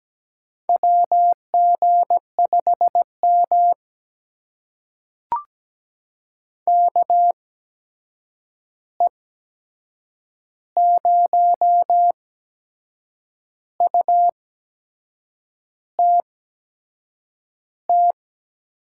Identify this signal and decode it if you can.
Morse code, heard as a keyed tone, about 17 wpm, spaced as for 6 wpm, 700 Hz. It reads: WG5M KE0UTT
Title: USA Call Signs - 8x Character Spacing - 17wpm